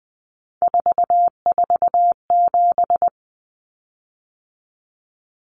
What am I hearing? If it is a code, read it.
Morse code, heard as a keyed tone, about 20 wpm, 700 Hz. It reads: 447
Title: Common Cut Numbers - 3-Digit 20wpm